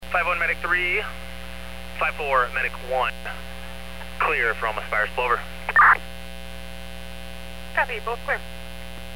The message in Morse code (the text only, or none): none